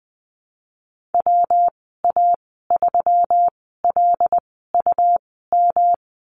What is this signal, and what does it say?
Morse code, heard as a keyed tone, about 20 wpm, spaced as for 15 wpm, 700 Hz. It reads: WA3LUM
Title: 154 Lesson 25g - Letter p - Call Signs 20wpm